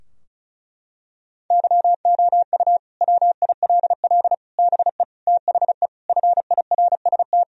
Morse code, heard as a keyed tone, about 36 wpm, 700 Hz. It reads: YOU WILL BE THE FIRST
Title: Sentences from Top 1000 Words - Review Part 4 - 35wpm